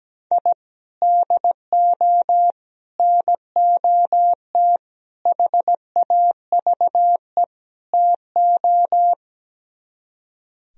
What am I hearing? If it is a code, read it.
Morse code, heard as a keyed tone, about 17 wpm, 700 Hz. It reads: I DO NOT HAVE TO
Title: Speed-Racing - Sets of 5 Words 17wpm